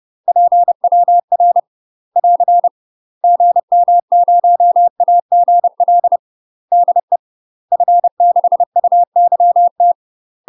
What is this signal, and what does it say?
Morse code, heard as a keyed tone, about 30 wpm, 700 Hz. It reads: PWR <AR> GM0AGL DE F6UYT
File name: UK MASTER QSOs TRAINING / 30WPM / UK MASTER QSOs 0035